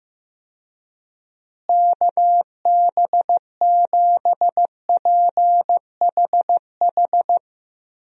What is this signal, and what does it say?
Morse code, heard as a keyed tone, about 16 wpm, 700 Hz. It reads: KB7PHH
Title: USA Call Signs - 6 Characters - Repeated 3x - 15wpm